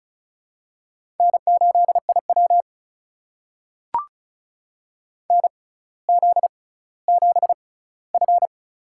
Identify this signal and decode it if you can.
Morse code, heard as a keyed tone, about 36 wpm, spaced as for 18 wpm, 700 Hz. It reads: N8IW NZ7F
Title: USA Call Signs - 4 Characters - 6x Character Spacing - 35wpm